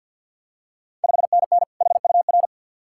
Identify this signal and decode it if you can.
Morse code, heard as a keyed tone, about 51 wpm, 690 Hz. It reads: 5NN HUR
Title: Ontario QSO Party Exchange 50wpm